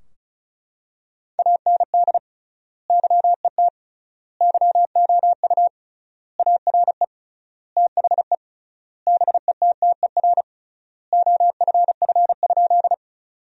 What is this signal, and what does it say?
Morse code, heard as a keyed tone, about 34 wpm, 700 Hz. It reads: AND YET YOU ARE THE BETTER OFF?
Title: 3x Word-Spacing - Sentences from Top 500 Words - 35wpm